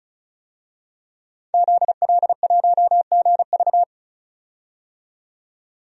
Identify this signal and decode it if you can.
Morse code, heard as a keyed tone, about 36 wpm, 700 Hz. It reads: ZL1GV
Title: DXCC-100 Call Signs - 5 Characters - 35wpm